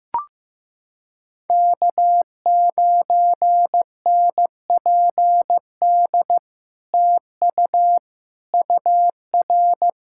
Morse code, heard as a keyed tone, about 15 wpm, 700 Hz. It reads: K9NPD TU UR
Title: SOTA QSO Practice 15wpm